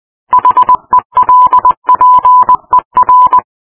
Morse code, heard as a keyed tone, about 20 wpm, 1000 Hz. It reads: HELPER